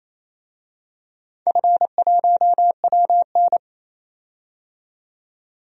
Morse code, heard as a keyed tone, about 28 wpm, 700 Hz. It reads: F1WN
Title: DXCC-100 Call Signs - 4 Characters - 28wpm